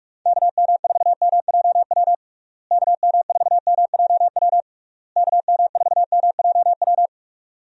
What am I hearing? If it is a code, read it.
Morse code, heard as a keyed tone, about 46 wpm, 700 Hz. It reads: KM4MJW KM4MJW KM4MJW
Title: USA Call Signs - 6 Characters - Repeated 3x - 45wpm